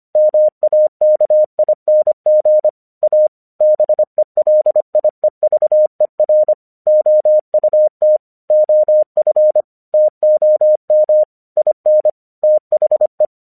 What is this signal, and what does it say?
Morse code, heard as a keyed tone, about 24 wpm, 610 Hz. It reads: MAKING A BELIEVER OUT OF TOM IN THE